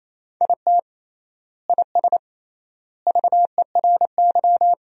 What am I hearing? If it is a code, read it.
Morse code, heard as a keyed tone, about 28 wpm, 700 Hz. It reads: IT IS VERY
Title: 3x Word-Spacing - Sentences from Top 500 Words - 28wpm